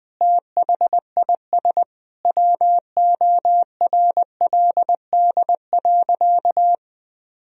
Morse code, heard as a keyed tone, about 20 wpm, 700 Hz. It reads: THIS WORLD.
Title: Sentences from Top 1000 Words - Review Part 1 - 20wpm